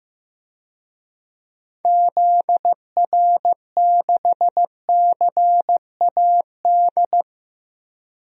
Morse code, heard as a keyed tone, about 16 wpm, 700 Hz. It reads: ZR6CAD
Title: DXCC-100 Call Signs - 6 Characters - Repeated 2x - 15wpm